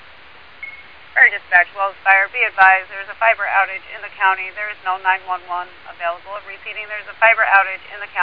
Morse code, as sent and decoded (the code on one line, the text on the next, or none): none